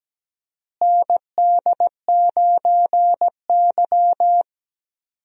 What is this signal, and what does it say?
Morse code, heard as a keyed tone, about 17 wpm, 700 Hz. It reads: ND9Y
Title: USA Call Signs - 4 Characters - Repeated 3x - 17wpm